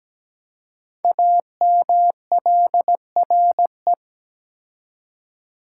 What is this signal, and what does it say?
Morse code, heard as a keyed tone, about 17 wpm, 700 Hz. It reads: AMLRE
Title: Code Groups - 5 Letters - 17wpm